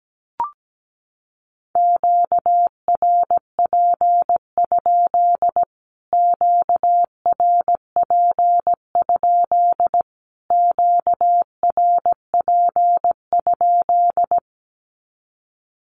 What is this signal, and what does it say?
Morse code, heard as a keyed tone, about 17 wpm, 700 Hz. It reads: QRP? QRP? QRP?
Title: Q-Codes Repeated 3x 17wpm